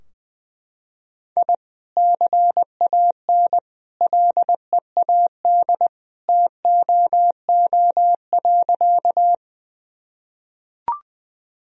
Morse code, heard as a keyed tone, about 20 wpm, 700 Hz. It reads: I CAN LEAD TOO.
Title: Sentences from Top 1000 Words - Review Part 2 - 20wpm